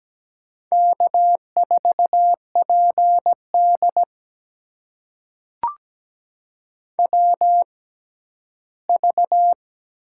Morse code, heard as a keyed tone, about 17 wpm, spaced as for 11 wpm, 700 Hz. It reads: K4PD WV
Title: USA Call Signs - 4 Characters - 6x Character Spacing - 17wpm